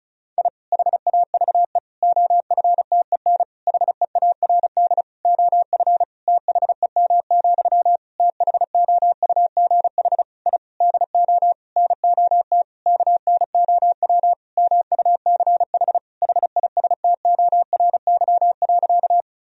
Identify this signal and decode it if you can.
Morse code, heard as a keyed tone, about 36 wpm, 700 Hz. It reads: I HAVE OFTEN HEARD OF THEM, THOUGH I DO NOT KNOW MUCH HISTORY.
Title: Sentences from Top 1000 Words - Review Part 2 - 35wpm